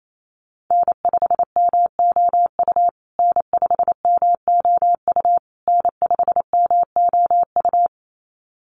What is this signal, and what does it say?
Morse code, heard as a keyed tone, about 28 wpm, 700 Hz. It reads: N5MOU N5MOU N5MOU
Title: USA Call Signs Repeated 3x 28wpm